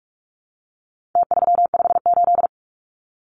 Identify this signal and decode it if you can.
Morse code, heard as a keyed tone, about 44 wpm, 700 Hz. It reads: T358
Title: Numbers - Contest Style - 4-Digit 45wpm